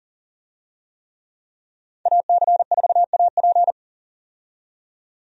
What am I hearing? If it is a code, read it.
Morse code, heard as a keyed tone, about 40 wpm, 700 Hz. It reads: AC4AP
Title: USA Call Signs - 5 Characters - 40wpm